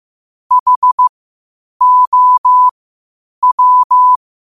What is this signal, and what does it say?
Morse code, heard as a keyed tone, about 15 wpm, spaced as for 8 wpm, 1000 Hz. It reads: HOW
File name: qotdfiles / 8wpm